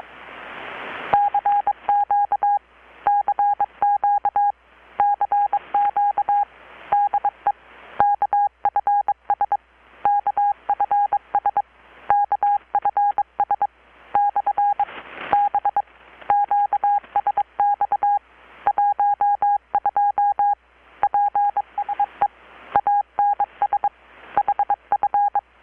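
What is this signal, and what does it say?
Morse code, heard as a keyed tone, about 22 wpm, 820 Hz. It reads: CQ CQ CQ DE KFS KFS KFS/B QSX 12 PSE ANS HF
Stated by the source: Morse from KPH's Night of Nights XX maritime radiotelegraphy event, received in Chiapas, Mexico about 3,800 km away on a Kenwood R-600 with a grounded 10 m long-wire antenna.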